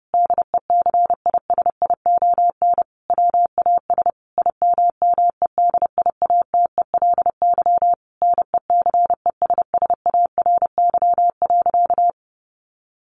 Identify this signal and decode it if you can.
Morse code, heard as a keyed tone, about 30 wpm, 700 Hz. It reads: DECISION WAS IMMEDIATELY NECESSARY.
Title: Sentences from Top 1100 Words 30wpm